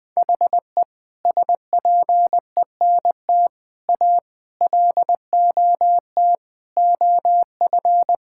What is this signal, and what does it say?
Morse code, heard as a keyed tone, about 20 wpm, 700 Hz. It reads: HE SPENT A LOT OF
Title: Sentences with 4-Word Phrases - Spoken Beginning - 20wpm